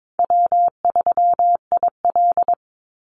Morse code, heard as a keyed tone, about 22 wpm, 700 Hz. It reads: W3IL
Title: Easy Call Signs - 22wpm